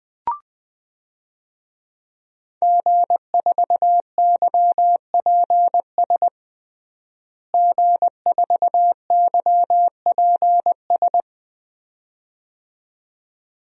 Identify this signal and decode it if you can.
Morse code, heard as a keyed tone, about 20 wpm, 700 Hz. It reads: G4YPS G4YPS
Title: DXCC-100 Call Signs - 5 Characters - Repeated 2x - 20wpm